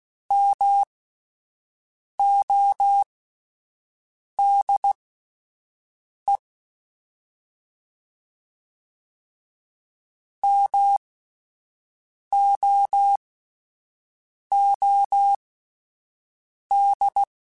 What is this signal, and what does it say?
Morse code, heard as a keyed tone, about 16 wpm, spaced as for 5 wpm, 800 Hz. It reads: MODE MOOD
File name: Alphabetical / Alphabetical mp3s / Ms @5